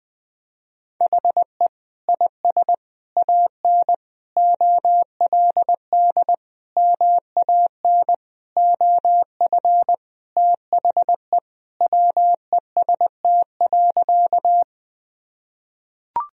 Morse code, heard as a keyed tone, about 20 wpm, 700 Hz. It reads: HE IS AN OLD MAN OF THE WEST.
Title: Sentences from Top 1000 Words - Review Part 2 - 20wpm